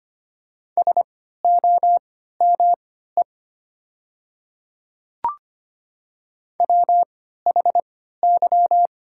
Morse code, heard as a keyed tone, about 25 wpm, spaced as for 14 wpm, 700 Hz. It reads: SOME WHY
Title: 3x Character Spacing - Top 200 Words - 25wpm